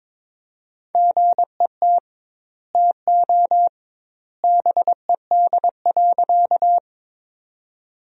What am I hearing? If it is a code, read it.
Morse code, heard as a keyed tone, about 22 wpm, 700 Hz. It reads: GET TO BED.
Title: 2x Word-Spacing - Sentences from Top 500 Words - 22wpm